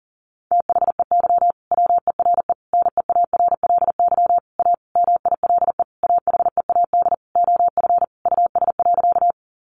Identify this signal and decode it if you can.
Morse code, heard as a keyed tone, about 40 wpm, 700 Hz. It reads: THEY WERE NEARLY A MILE AHEAD OF US.
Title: Sentences from Top 1300 Words 40wpm